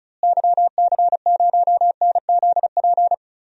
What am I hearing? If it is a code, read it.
Morse code, heard as a keyed tone, about 34 wpm, 700 Hz. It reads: YC0NZP
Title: DXCC-100 Call Signs - 6 Characters - Repeated 2x - 35wpm